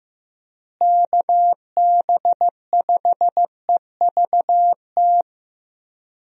Morse code, heard as a keyed tone, about 15 wpm, 700 Hz. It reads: KB5EVT
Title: USA Call Signs - 6 Characters - Repeated 3x - 15wpm